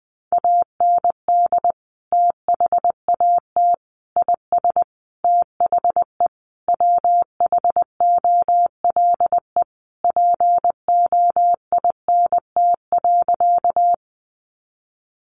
Morse code, heard as a keyed tone, about 19 wpm, 700 Hz. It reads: AND THAT IS THE WHOLE POINT.
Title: Sentences from Top 400 Words 20wpm